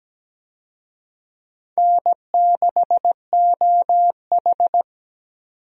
Morse code, heard as a keyed tone, about 17 wpm, 700 Hz. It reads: N6OH
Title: USA Call Signs - 4 Characters - 17wpm